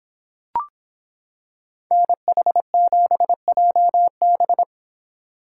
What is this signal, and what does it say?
Morse code, heard as a keyed tone, about 25 wpm, 700 Hz. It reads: NH7JB